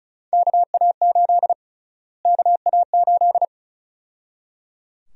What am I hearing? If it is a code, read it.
Morse code, heard as a keyed tone, about 34 wpm, 700 Hz. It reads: KA8 KA8
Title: USA Call Sign Prefixes - Repeated 2x - 35wpm